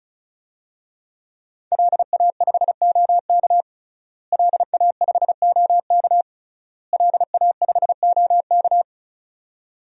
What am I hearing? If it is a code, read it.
Morse code, heard as a keyed tone, about 34 wpm, 700 Hz. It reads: LA5OK LA5OK LA5OK
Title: DXCC-100 Call Signs - 5 Characters - Repeated 3x - 35wpm